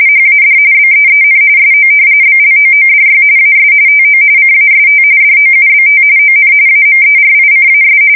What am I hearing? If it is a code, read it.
RTTY, about 45.45 baud. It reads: HAD BEEN SO BLOODY AS TO CAST DOUBT ON AMERICAS
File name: fsk170 / battleshipnj